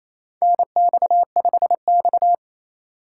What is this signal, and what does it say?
Morse code, heard as a keyed tone, about 29 wpm, 700 Hz. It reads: NX5X
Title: USA Call Signs - 4 Characters - 6x Character Spacing - 28wpm